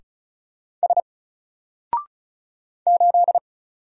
Morse code, heard as a keyed tone, about 36 wpm, 700 Hz. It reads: S 8
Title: Single Character (Letter:Number:Punctuation:Prosign) 35wpm